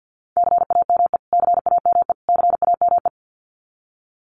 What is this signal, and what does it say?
Morse code, heard as a keyed tone, about 50 wpm, 700 Hz. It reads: CAME CAME CAME
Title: Top 300 Words Repeated 50wpm